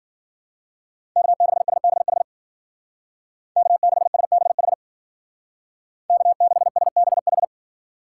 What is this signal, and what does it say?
Morse code, heard as a keyed tone, about 48 wpm, 700 Hz. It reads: K6SBH K6SBH K6SBH
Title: Speed Racing - USA Call Signs 40wpm